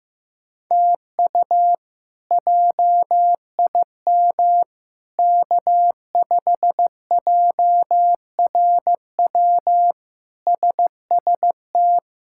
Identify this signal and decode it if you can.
Morse code, heard as a keyed tone, about 15 wpm, 700 Hz. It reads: TU JIM K5JRW SST
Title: K1USN SST QSO Practice - 15wpm